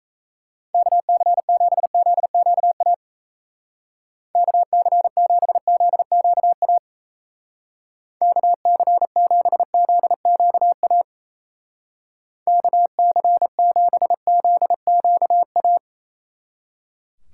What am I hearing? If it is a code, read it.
Morse code, heard as a keyed tone, about 34 wpm, 700 Hz. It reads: KC7ZQA KC7ZQA KC7ZQA KC7ZQA
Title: Speed Racing - USA 2x3 Call Signs 28wpm